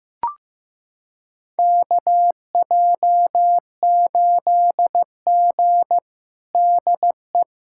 Morse code, heard as a keyed tone, about 15 wpm, 700 Hz. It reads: KJ8G DE
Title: USA Callsign DE Callsign - 4 Characters - 15wpm